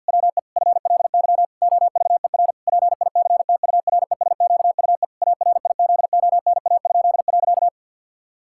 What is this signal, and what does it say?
Morse code, heard as a keyed tone, about 49 wpm, 700 Hz. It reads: WE FLY OVER PICTURESQUE ARIZONA?.
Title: Pangram Challenge v2 - 50wpm